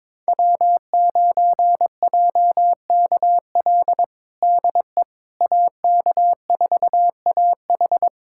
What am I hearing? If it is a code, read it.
Morse code, heard as a keyed tone, about 22 wpm, 700 Hz. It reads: W9JKL DE AK4AH
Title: USA Callsign DE Callsign 22wpm